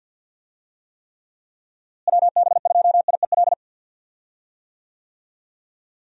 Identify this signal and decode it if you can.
Morse code, heard as a keyed tone, about 49 wpm, 700 Hz. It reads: WB2IEL
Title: USA Call Signs - 6 Characters - Repeated 2x - 50wpm